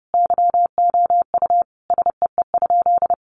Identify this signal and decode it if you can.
Morse code, heard as a keyed tone, about 30 wpm, 700 Hz. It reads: YOU SEE?
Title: Sentences from Top 100 Words 30wpm